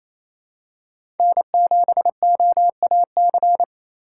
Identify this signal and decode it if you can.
Morse code, heard as a keyed tone, about 28 wpm, 700 Hz. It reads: N7OAC
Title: USA Call Signs - 5 Characters - 2x Character Spacing - 28wpm